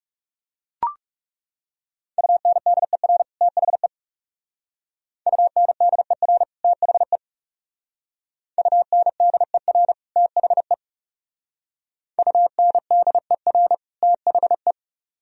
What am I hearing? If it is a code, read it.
Morse code, heard as a keyed tone, about 39 wpm, 700 Hz. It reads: UNDER THE UNDER THE UNDER THE UNDER THE